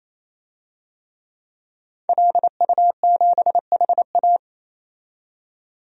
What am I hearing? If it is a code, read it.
Morse code, heard as a keyed tone, about 28 wpm, 700 Hz. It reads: LU7HA
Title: DXCC-100 Call Signs - 5 Characters - 28wpm